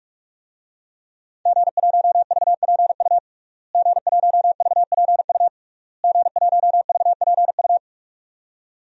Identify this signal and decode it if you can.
Morse code, heard as a keyed tone, about 46 wpm, 700 Hz. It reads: G1VPU G1VPU G1VPU
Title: DXCC-100 Call Signs - 5 Characters - Repeated 3x - 45wpm